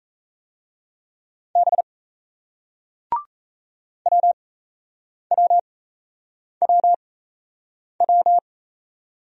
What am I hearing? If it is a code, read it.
Morse code, heard as a keyed tone, about 38 wpm, spaced as for 8 wpm, 700 Hz. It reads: B WWWW